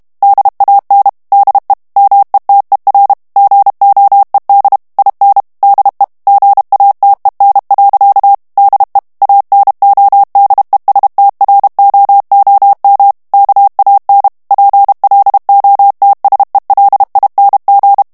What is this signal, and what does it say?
Morse code, heard as a keyed tone, about 32 wpm, 800 Hz. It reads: DAN DE METER GOED IN DE GATEN. DE ANODESTROOM KAN PLOTSELING
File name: HOLLANDAIS texte clair 2 vitesse 30 mots minute